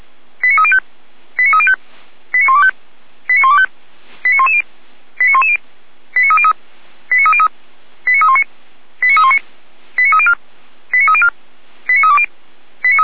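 Five-tone selective call (ZVEI), heard as quick runs of five tones, tones of about 80 ms. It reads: 89305 89305 89125 89125 89110 89110 89303 89303 89319 89319 89304 89304 89320
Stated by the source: ZVEI five-tone selective call received by SDR.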